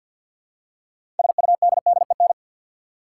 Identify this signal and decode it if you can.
Morse code, heard as a keyed tone, about 53 wpm, 700 Hz. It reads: SUDDEN
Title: Six Letter Words - 50wpm